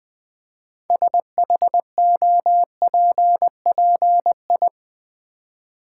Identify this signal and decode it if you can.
Morse code, heard as a keyed tone, about 20 wpm, 700 Hz. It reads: SHOPPI
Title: Build A Word - 2 Letters At A Time - 8-Letter Words - 20wpm